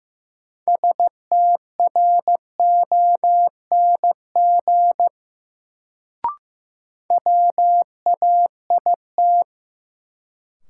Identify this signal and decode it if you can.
Morse code, heard as a keyed tone, about 15 wpm, 700 Hz. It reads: STRONG WAIT
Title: Sentences from Top 500 Words - New Word Drill - 15wpm